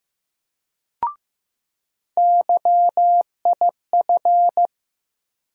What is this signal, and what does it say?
Morse code, heard as a keyed tone, about 15 wpm, 700 Hz. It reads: YIF
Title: USA Call Sign Postfixes - 3 Letters 15wpm